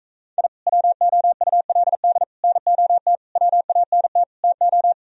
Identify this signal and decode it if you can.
Morse code, heard as a keyed tone, about 34 wpm, 700 Hz. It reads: I WOULD NOT WANT TO
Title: Speed-Racing - Sets of 5 Words 28wpm